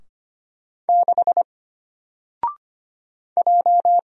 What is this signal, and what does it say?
Morse code, heard as a keyed tone, about 25 wpm, 700 Hz. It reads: 6 J
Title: Single Letter-Number 25wpm